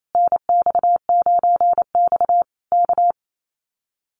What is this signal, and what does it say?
Morse code, heard as a keyed tone, about 28 wpm, 700 Hz. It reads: NX9X K
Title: Common QSO Phrases Encore 28wpm